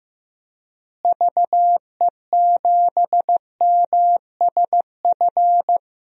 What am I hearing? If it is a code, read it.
Morse code, heard as a keyed tone, about 15 wpm, 700 Hz. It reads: VE7MSF
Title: DXCC-100 Call Signs - 6 Characters - Repeated 2x - 15wpm